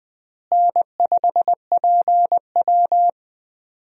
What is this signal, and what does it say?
Morse code, heard as a keyed tone, about 21 wpm, 700 Hz. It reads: N5PW
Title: USA Call Signs - 4 Characters - Repeated 2x - 20wpm